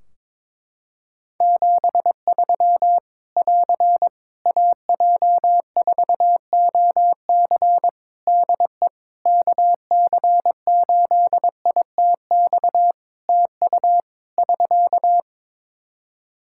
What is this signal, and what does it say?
Morse code, heard as a keyed tone, about 22 wpm, 700 Hz. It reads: 73 <AR> AJ4OC DE KC8ITX TU <SK>
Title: Practice QSO - Exchange 4 - Ending - 22wpm